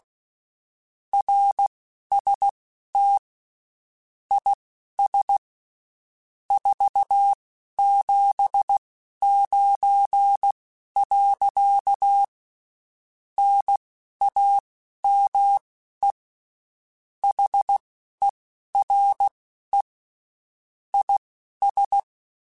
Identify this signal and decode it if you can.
Morse code, heard as a keyed tone, about 16 wpm, spaced as for 11 wpm, 800 Hz. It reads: RST IS 479. NAME HERE IS
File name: (spaced) / QSOs Spaced-mp3s / kc7sac @15 sp